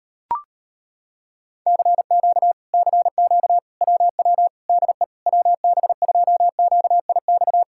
Morse code, heard as a keyed tone, about 35 wpm, 700 Hz. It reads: CQ CQ WW DE WB2QIX